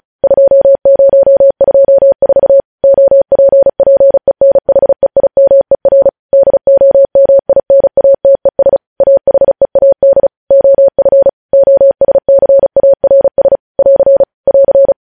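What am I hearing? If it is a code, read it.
Morse code, heard as a keyed tone, about 36 wpm, 550 Hz. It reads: 2024 OPPENHEIMER DOMINATES AHEAD OF OSCARS <AR> <AR>
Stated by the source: Morse code news headlines practice sent at 35 wpm.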